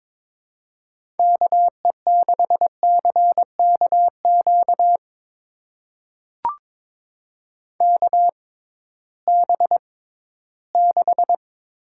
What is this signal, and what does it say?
Morse code, heard as a keyed tone, about 21 wpm, spaced as for 14 wpm, 700 Hz. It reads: KE6CKQ KB6
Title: USA Call Signs - 6 Characters - 6x Character Spacing - 22wpm